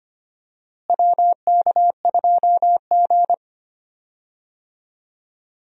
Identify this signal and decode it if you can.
Morse code, heard as a keyed tone, about 25 wpm, 700 Hz. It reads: WK2G
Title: USA Call Signs - 4 Characters - Repeated 3x - 25wpm